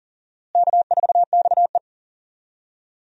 Morse code, heard as a keyed tone, about 40 wpm, 700 Hz. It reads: K4XE
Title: USA Call Signs - 4 Characters - Repeated 2x - 40wpm